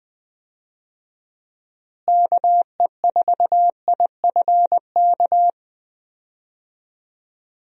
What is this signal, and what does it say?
Morse code, heard as a keyed tone, about 21 wpm, 700 Hz. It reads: KE4IFK
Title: USA Call Signs - 6 Characters - Repeated 3x - 20wpm